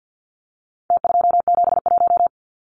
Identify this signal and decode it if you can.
Morse code, heard as a keyed tone, about 48 wpm, 700 Hz. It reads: T271
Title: Numbers - Contest Style - 4-Digit 50wpm